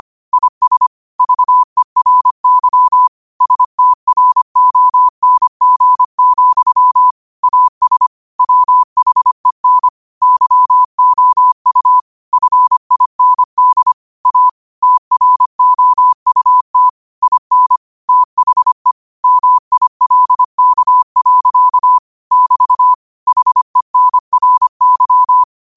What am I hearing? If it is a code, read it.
Morse code, heard as a keyed tone, about 25 wpm, 1000 Hz. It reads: IS VERY STRONG, AS WHEN YOU FIND A TROUT IN THE MILK. <BT> HENRY